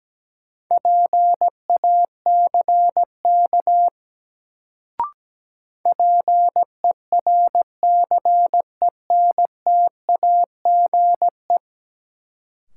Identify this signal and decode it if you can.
Morse code, heard as a keyed tone, about 17 wpm, 700 Hz. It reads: PACK PERCENTAGE
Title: Sentences from Top 1300 Words - New Word Drill - 17wpm